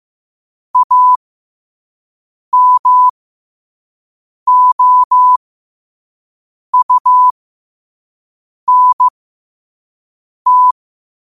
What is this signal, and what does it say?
Morse code, heard as a keyed tone, about 15 wpm, spaced as for 5 wpm, 1000 Hz. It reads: AMOUNT
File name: qotdfiles / 5wpm